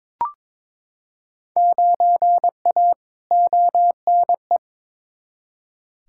Morse code, heard as a keyed tone, about 21 wpm, 700 Hz. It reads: 9A ONE
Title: Field Day Exchanges Encore - 22wpm